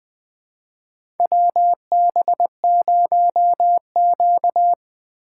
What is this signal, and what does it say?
Morse code, heard as a keyed tone, about 20 wpm, 700 Hz. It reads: WB0Q
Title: USA Call Signs - 4 Characters - 3x Character Spacing - 20wpm